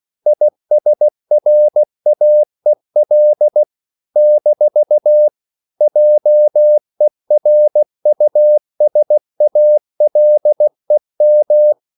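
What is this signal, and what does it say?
Morse code, heard as a keyed tone, about 16 wpm, 600 Hz. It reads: ISRAEL - JERUSALEM
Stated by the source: CW practice of national capitals at 16 words per minute, from ON6ZQ.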